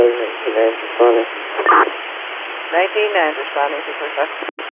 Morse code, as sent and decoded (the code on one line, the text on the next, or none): none